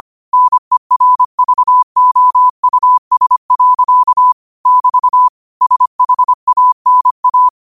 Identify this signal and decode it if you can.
Morse code, heard as a keyed tone, about 25 wpm, 1000 Hz. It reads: NERVOUS. <BT> SHANA
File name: qotdfiles / 25wpm